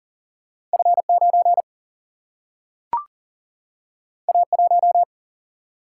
Morse code, heard as a keyed tone, about 40 wpm, 700 Hz. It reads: F9 A1
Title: Single-Letter Single-Number 40wpm